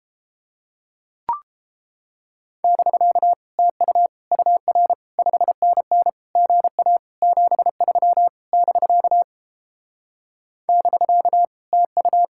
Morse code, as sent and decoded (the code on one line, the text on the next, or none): -...-.- / - ..- / ..- .-. / ..... -. -. / --. .- / --... ...-- / -...-.- / -...-.- / - ..-
<BK> TU UR 5NN GA 73 <BK> <BK> TU